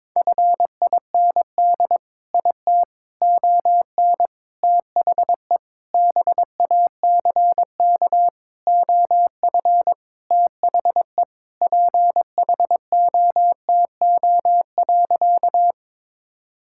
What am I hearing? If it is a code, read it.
Morse code, heard as a keyed tone, about 22 wpm, 700 Hz. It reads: FIND IT ON THE BACK OF THE PHOTO.
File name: practice / 22wpm / Sentences with 4-Word Phrases - 22wpm